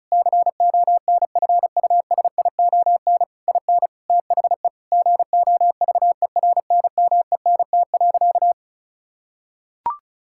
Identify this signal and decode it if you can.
Morse code, heard as a keyed tone, about 36 wpm, 690 Hz. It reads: CONFUSION IN THE GOVERNMENT.
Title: Sentences from Top 1600 Words 35wpm